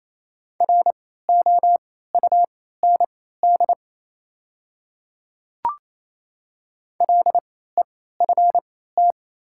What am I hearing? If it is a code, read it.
Morse code, heard as a keyed tone, about 28 wpm, spaced as for 16 wpm, 700 Hz. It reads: ROUND LEFT
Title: 3x Character Spacing - Top 300 Words - 28wpm